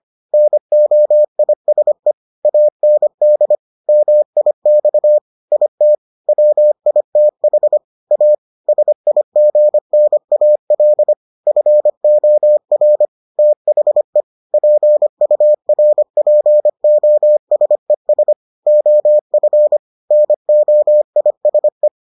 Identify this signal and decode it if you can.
Morse code, heard as a keyed tone, about 25 wpm, 600 Hz. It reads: NOISE AND MIX IT WITH A SIGNAL FOR THE PURPOSES OF NOISE